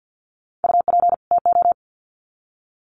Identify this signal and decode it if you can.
Morse code, heard as a keyed tone, about 50 wpm, 700 Hz. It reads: UP TO
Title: Sets of 2 Words 50wpm